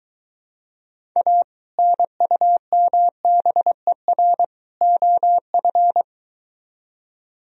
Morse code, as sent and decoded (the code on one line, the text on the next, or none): .- / -. ..- -- -... .-. / --- ..-.
A NUMBER OF